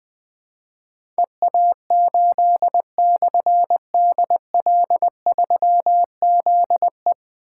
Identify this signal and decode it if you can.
Morse code, heard as a keyed tone, about 20 wpm, 700 Hz. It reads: EA8/DL3ZE
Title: Difficult Call Signs 20wpm